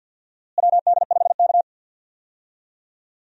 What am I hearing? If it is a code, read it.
Morse code, heard as a keyed tone, about 51 wpm, 700 Hz. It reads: WD5K
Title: USA Call Signs - 4 Characters - Repeated 3x - 50wpm